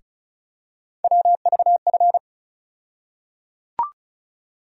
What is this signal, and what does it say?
Morse code, heard as a keyed tone, about 36 wpm, 700 Hz. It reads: WVF
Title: USA Call Sign Postfixes - 3 Letters 35wpm